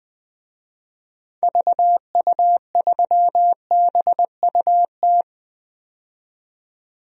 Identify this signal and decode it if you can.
Morse code, heard as a keyed tone, about 21 wpm, 700 Hz. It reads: VU3BUT
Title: DXCC-100 Call Signs - 6 Characters - 20wpm